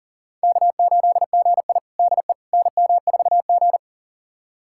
Morse code, heard as a keyed tone, about 38 wpm, 700 Hz. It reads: K8GI DE NM4G
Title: USA Callsign DE Callsign - 4 Characters - 40wpm